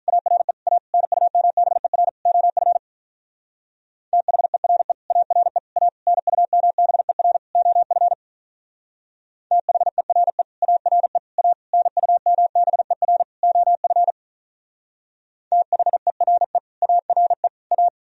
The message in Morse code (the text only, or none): ARE A NUMBER OF THERE ARE A NUMBER OF THERE ARE A NUMBER OF THERE ARE A